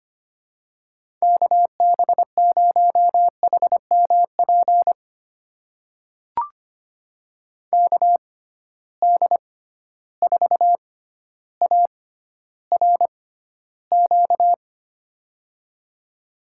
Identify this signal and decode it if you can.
Morse code, heard as a keyed tone, about 25 wpm, spaced as for 13 wpm, 700 Hz. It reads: KB0HMP KD4ARQ
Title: USA Call Signs - 6 Characters - 6x Character Spacing - 25wpm